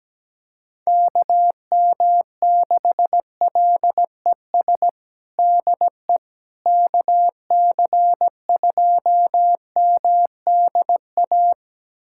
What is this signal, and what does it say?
Morse code, heard as a keyed tone, about 17 wpm, 700 Hz. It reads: KM6LES DE KC2MDA
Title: USA Callsign DE Callsign - 6 Characters - 17wpm